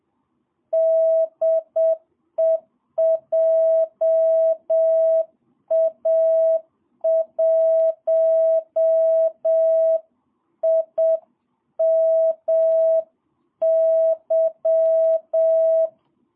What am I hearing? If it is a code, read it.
Morse code, heard as a keyed tone, about 7 wpm, 660 Hz. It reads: DE JA1IMY
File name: jh8cbh / 2022cw